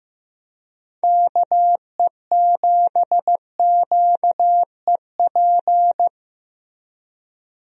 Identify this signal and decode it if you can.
Morse code, heard as a keyed tone, about 16 wpm, 700 Hz. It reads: KE7QEP
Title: USA Call Signs - 6 Characters - Repeated 3x - 15wpm